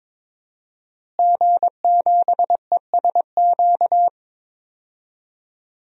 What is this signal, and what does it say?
Morse code, heard as a keyed tone, about 22 wpm, 700 Hz. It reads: G7ESQ
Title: DXCC-100 Call Signs - 5 Characters - Repeated 3x - 22wpm